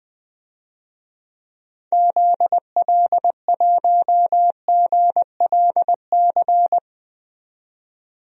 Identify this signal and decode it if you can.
Morse code, heard as a keyed tone, about 21 wpm, 700 Hz. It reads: ZL1GLC
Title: DXCC-100 Call Signs - 6 Characters - 20wpm